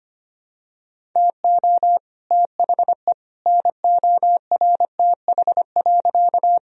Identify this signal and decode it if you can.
Morse code, heard as a keyed tone, about 25 wpm, 700 Hz. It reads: TO THE NORTH.
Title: Sentences from Top 1000 Words - Review Part 1 - 25wpm